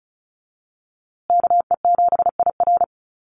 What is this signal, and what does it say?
Morse code, heard as a keyed tone, about 34 wpm, 700 Hz. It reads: KE7IR
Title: USA Call Signs Repeated 2x 35wpm